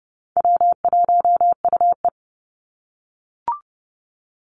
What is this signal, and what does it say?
Morse code, heard as a keyed tone, about 30 wpm, 700 Hz. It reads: W1UE
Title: Call Signs - 30wpm